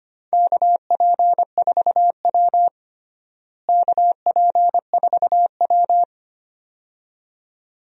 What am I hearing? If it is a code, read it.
Morse code, heard as a keyed tone, about 25 wpm, 700 Hz. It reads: KP4W KP4W
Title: DXCC-100 Call Signs - 4 Characters - Repeated 2x - 25wpm